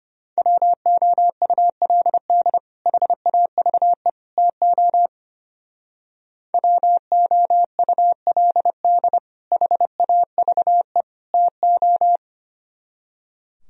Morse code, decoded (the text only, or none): WOULD HAVE TO WOULD HAVE TO